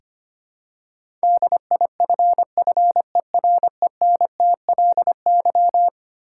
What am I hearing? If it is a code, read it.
Morse code, heard as a keyed tone, about 25 wpm, 700 Hz. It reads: DIFFERENTLY
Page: morse-code-ninja-practice-25wpm